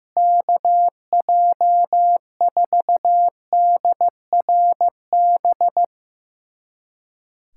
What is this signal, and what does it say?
Morse code, heard as a keyed tone, about 15 wpm, 700 Hz. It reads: KJ4DRB
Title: USA Callsign DE Callsign 15wpm